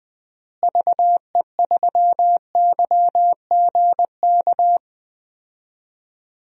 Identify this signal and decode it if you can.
Morse code, heard as a keyed tone, about 21 wpm, 700 Hz. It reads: VE3YGK
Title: DXCC-100 Call Signs - 6 Characters - Repeated 3x - 20wpm